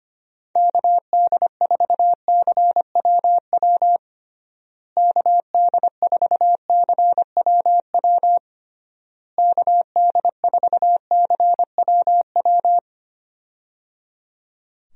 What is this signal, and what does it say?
Morse code, heard as a keyed tone, about 25 wpm, 700 Hz. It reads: KD4CWW KD4CWW KD4CWW
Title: USA Call Signs - 6 Characters - Repeated 3x - 25wpm